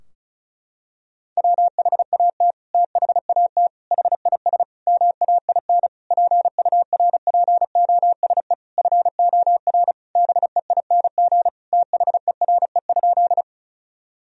Morse code, heard as a keyed tone, about 36 wpm, 700 Hz. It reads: WHAT THAT HIS MAIN PURPOSE FOR BEING THERE?
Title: Sentences from Top 1000 Words - Review Part 2 - 35wpm